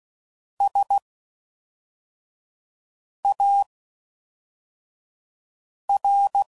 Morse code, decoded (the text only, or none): S A R